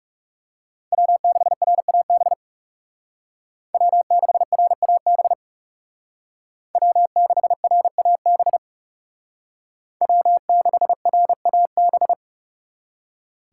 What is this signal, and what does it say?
Morse code, heard as a keyed tone, about 38 wpm, 700 Hz. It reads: W6RAB W6RAB W6RAB W6RAB